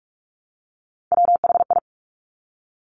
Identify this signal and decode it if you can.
Morse code, heard as a keyed tone, about 46 wpm, 690 Hz. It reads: WHI
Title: Top Three Letter Combinations in Words 45wpm